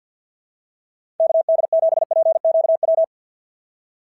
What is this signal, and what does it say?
Morse code, heard as a keyed tone, about 46 wpm, 650 Hz. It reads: KD7PQW
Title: SOTA QSO Practice 50wpm